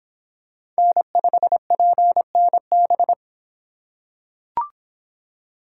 Morse code, heard as a keyed tone, about 25 wpm, 700 Hz. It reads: N5PNB